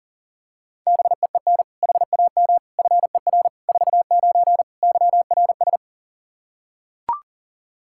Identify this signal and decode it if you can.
Morse code, heard as a keyed tone, about 37 wpm, 700 Hz. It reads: BEEN HAM FER 49 YRS
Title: Practice QSO - Exchange 3 - More Info - 40wpm